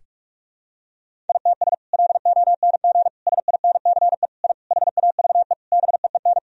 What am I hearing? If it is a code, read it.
Morse code, heard as a keyed tone, about 41 wpm, 700 Hz. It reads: ITS LONG SINCE I HAVE BEEN
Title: Sentences from Top 1000 Words - Review Part 1 - 45wpm